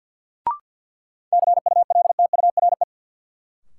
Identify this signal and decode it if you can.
Morse code, heard as a keyed tone, about 53 wpm, 700 Hz. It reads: CULTURE
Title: Sentences from Top 1000 Words - Review Part 6 - New Word Drill - 50wpm